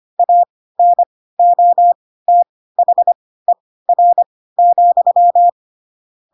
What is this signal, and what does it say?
Morse code, heard as a keyed tone, about 25 wpm, spaced as for 16 wpm, 700 Hz. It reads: ANOTHER,
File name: Pinocchio / Pinocchio ch3 10 25 ik1hge